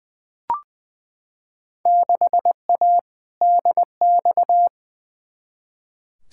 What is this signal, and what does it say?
Morse code, heard as a keyed tone, about 20 wpm, 700 Hz. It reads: 6A DX
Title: Field Day Exchanges Encore - 20wpm